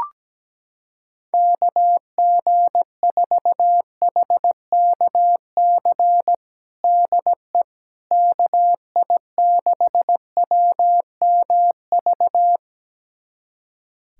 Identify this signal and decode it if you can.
Morse code, heard as a keyed tone, about 17 wpm, 700 Hz. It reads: KG4HKC DE KI6WMV
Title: USA Callsign DE Callsign 17wpm